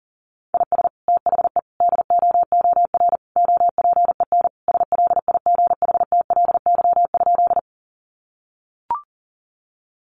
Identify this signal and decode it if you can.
Morse code, heard as a keyed tone, about 37 wpm, 700 Hz. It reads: IS THE DOOR OPEN SLIGHTLY?
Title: Sentences from Top 1300 Words 40wpm